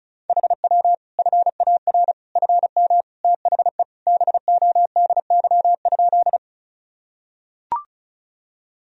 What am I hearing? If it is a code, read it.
Morse code, heard as a keyed tone, about 34 wpm, 700 Hz. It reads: HW FAR FM THE BODY?
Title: Sentences from Top 400 Words and Abbreviations 35wpm